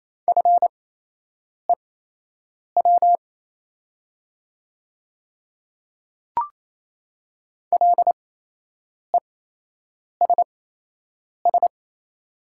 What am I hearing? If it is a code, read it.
Morse code, heard as a keyed tone, about 28 wpm, spaced as for 8 wpm, 700 Hz. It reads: FEW LESS
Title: 8x Character Spacing - Top 500 Words - 28wpm